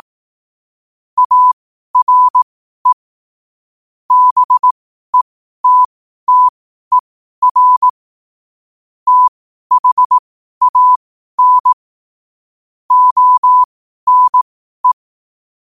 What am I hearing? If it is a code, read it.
Morse code, heard as a keyed tone, about 18 wpm, spaced as for 12 wpm, 1000 Hz. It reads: ARE BETTER THAN ONE
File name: qotdfiles / 12wpm